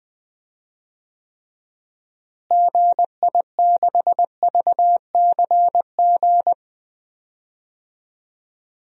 Morse code, heard as a keyed tone, about 20 wpm, 700 Hz. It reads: GI6VCG
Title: DXCC-100 Call Signs - 6 Characters - Repeated 2x - 20wpm